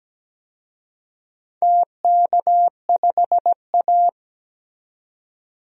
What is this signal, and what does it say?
Morse code, heard as a keyed tone, about 17 wpm, 700 Hz. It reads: TK5A
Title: DXCC-100 Call Signs - 4 Characters - 17wpm